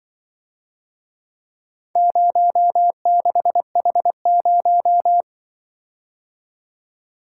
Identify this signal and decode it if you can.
Morse code, heard as a keyed tone, about 23 wpm, 700 Hz. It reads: 06H0
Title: Speed-Racing - Four Letter-Number Combinations - 20wpm